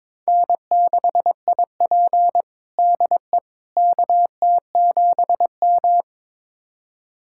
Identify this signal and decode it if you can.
Morse code, heard as a keyed tone, about 22 wpm, 700 Hz. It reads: N6IP DE KT7M
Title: USA Callsign DE Callsign - 4 Characters - 22wpm